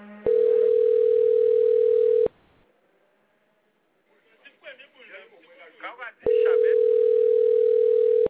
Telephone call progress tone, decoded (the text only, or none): ringback tone